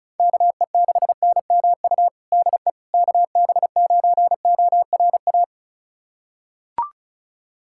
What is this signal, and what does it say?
Morse code, heard as a keyed tone, about 36 wpm, 700 Hz. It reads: KE6NMU DE KB9ORA
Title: USA Callsign DE Callsign 35wpm